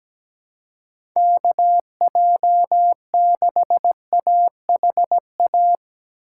Morse code, heard as a keyed tone, about 17 wpm, 700 Hz. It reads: KJ6AHA
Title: USA Call Signs - 2x Character Spacing - 17wpm